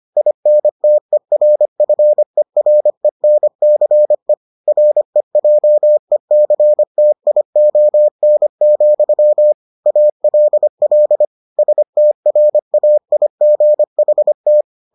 Morse code, heard as a keyed tone, about 25 wpm, 600 Hz. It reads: INTERFERENCE REJECTION, ALL STRAIGHT